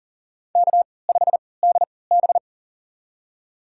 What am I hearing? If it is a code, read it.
Morse code, heard as a keyed tone, about 40 wpm, spaced as for 23 wpm, 700 Hz. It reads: K5DB
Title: USA Call Signs - 4 Characters - 3x Character Spacing - 40wpm